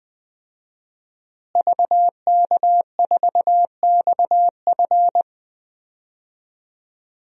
Morse code, heard as a keyed tone, about 20 wpm, 700 Hz. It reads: VK4XF
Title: DXCC-100 Call Signs - 5 Characters - Repeated 2x - 20wpm